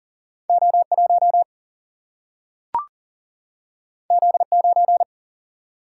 Morse code, heard as a keyed tone, about 40 wpm, 700 Hz. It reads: O1 Z9
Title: Single-Letter Single-Number 40wpm